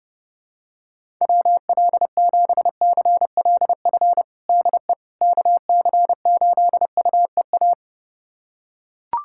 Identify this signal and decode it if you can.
Morse code, heard as a keyed tone, about 30 wpm, 700 Hz. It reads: WL7CLF DE KC8UEA
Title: USA Callsign DE Callsign - 6 Characters - 30wpm